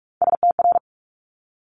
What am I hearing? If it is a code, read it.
Morse code, heard as a keyed tone, about 46 wpm, 700 Hz. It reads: STR
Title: Multi-letter Phonemes Quick Fire 45wpm